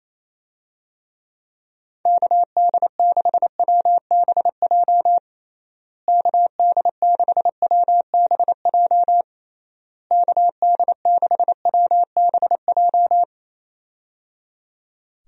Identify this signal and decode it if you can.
Morse code, heard as a keyed tone, about 28 wpm, 700 Hz. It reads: KD6WBJ KD6WBJ KD6WBJ
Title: USA Call Signs - 6 Characters - Repeated 3x - 28wpm